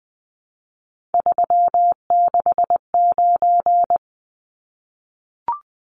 Morse code, heard as a keyed tone, about 20 wpm, 700 Hz. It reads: 369
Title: Numbers - 3-Digit 20wpm